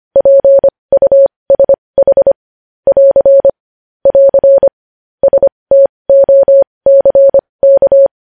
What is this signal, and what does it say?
Morse code, heard as a keyed tone, about 25 wpm, spaced as for 20 wpm, 550 Hz. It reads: PUSH <AR> <AR> STOCK
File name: podcast / News headlines-1x-25x20